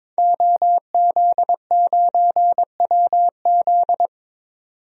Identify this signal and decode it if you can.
Morse code, heard as a keyed tone, about 22 wpm, 700 Hz. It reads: OZ9WZ
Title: DXCC-100 Call Signs - 5 Characters - 22wpm